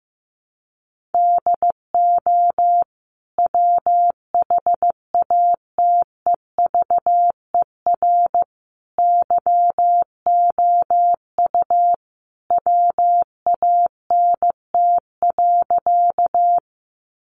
Morse code, heard as a keyed tone, about 15 wpm, 700 Hz. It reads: DO WHATEVER YOU WANT.
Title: Sentences from Top 1100 Words 15wpm